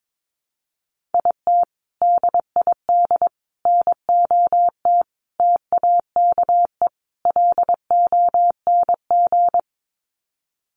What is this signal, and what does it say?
Morse code, heard as a keyed tone, about 22 wpm, 700 Hz. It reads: IT DID NOT TAKE LONG
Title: Sets of 5 Words Encore2 22wpm